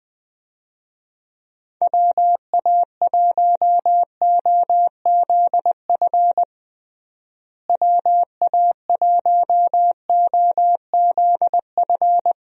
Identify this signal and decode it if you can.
Morse code, heard as a keyed tone, about 20 wpm, 700 Hz. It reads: WA1OZF WA1OZF
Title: USA Call Signs - 6 Characters - Repeated 3x - 20wpm